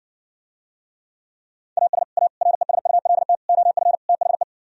Morse code, heard as a keyed tone, about 60 wpm, 700 Hz. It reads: AS A RESULT OF THE